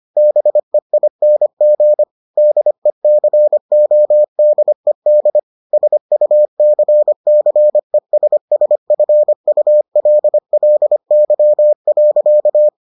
Morse code, heard as a keyed tone, about 25 wpm, 600 Hz. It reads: BEING DECODED SUCCESSFULLY.